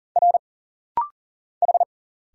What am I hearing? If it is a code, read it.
Morse code, heard as a keyed tone, about 40 wpm, 700 Hz. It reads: R H
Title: Single Letter-Number - Rapid-Fire 40wpm